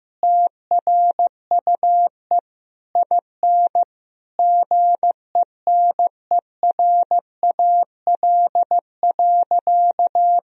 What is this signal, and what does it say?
Morse code, heard as a keyed tone, about 15 wpm, 700 Hz. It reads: TRUE IN GENERAL.
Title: Sentences from Top 1000 Words - Review Part 3 - 15wpm